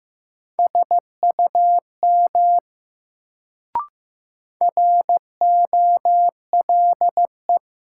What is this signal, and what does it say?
Morse code, heard as a keyed tone, about 15 wpm, 700 Hz. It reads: SUM ROLE
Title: Sentences from Top 1000 Words - Review Part 6 - New Word Drill - 15wpm